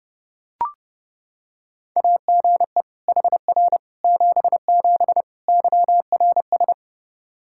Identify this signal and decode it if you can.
Morse code, heard as a keyed tone, about 30 wpm, 700 Hz. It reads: AGE HR 77 YRS
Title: Practice QSO - Exchange 3 - More Info - 30wpm